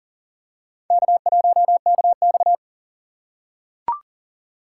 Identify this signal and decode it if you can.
Morse code, heard as a keyed tone, about 37 wpm, 700 Hz. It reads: K1KX
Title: USA Call Signs - 4 Characters - 8x Character Spacing - 40wpm